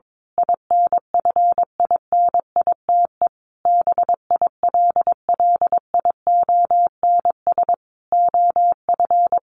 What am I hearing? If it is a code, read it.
Morse code, heard as a keyed tone, about 22 wpm, 700 Hz. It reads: INFINITE BILLIONS OF
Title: A Sound Of Thunder 22wpm